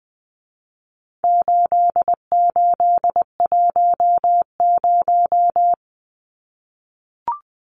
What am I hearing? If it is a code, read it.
Morse code, heard as a keyed tone, about 20 wpm, 700 Hz. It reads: 8810
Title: Numbers - 4-Digit 20wpm